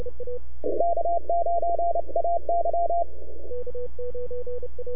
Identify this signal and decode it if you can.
Morse code, heard as a keyed tone, about 28 wpm, 690 Hz. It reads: K9AY
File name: NAQPCW-8-2009 / W1UJ NAQP CW 09 1930Z